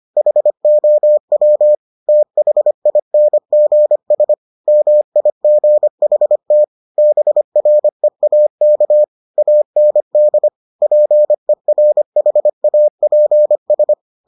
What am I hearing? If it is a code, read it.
Morse code, heard as a keyed tone, about 25 wpm, 600 Hz. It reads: HOW THINGS MIGHT BREAK AND PERHAPS